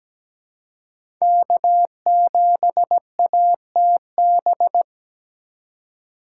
Morse code, heard as a keyed tone, about 17 wpm, 700 Hz. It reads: K7ATB
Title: USA Call Signs - 2x Character Spacing - 17wpm